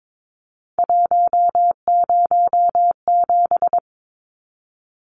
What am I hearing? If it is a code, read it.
Morse code, heard as a keyed tone, about 22 wpm, 700 Hz. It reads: 107
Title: Numbers - 3-Digit 22wpm